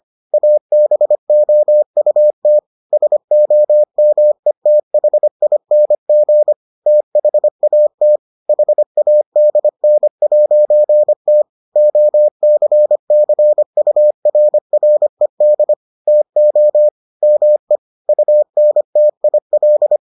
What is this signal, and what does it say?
Morse code, heard as a keyed tone, about 25 wpm, 600 Hz. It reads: ABOUT SOMETHING THAT HADN'T OCCURRED TO ME UNTIL